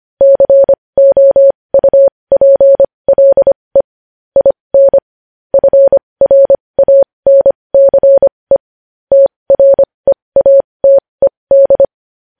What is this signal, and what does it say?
Morse code, heard as a keyed tone, about 25 wpm, spaced as for 20 wpm, 550 Hz. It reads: COUPLE IN FRANCE TREATED
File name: podcast / News headlines-1x-25x20